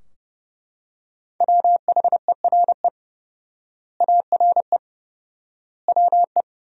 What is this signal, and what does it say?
Morse code, heard as a keyed tone, about 30 wpm, 700 Hz. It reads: WHERE ARE WE
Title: 4x Word-Spacing - Sentences from Top 200 Words - 30wpm